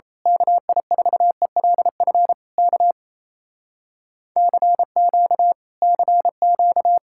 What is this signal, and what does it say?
Morse code, heard as a keyed tone, about 26 wpm, 700 Hz. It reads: KI4ELF K CQ CQ
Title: Speed-Racing Common QSO Phrases - 28wpm